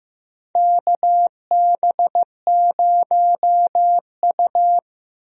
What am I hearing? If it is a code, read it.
Morse code, heard as a keyed tone, about 15 wpm, 700 Hz. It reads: KB0U
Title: USA Call Signs - 4 Characters - Repeated 2x - 15wpm